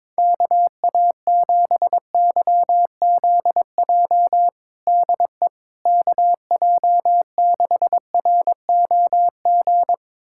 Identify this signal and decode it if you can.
Morse code, heard as a keyed tone, about 22 wpm, 700 Hz. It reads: KA7YZJ DE KJ6ROG
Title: USA Callsign DE Callsign - 6 Characters - 22wpm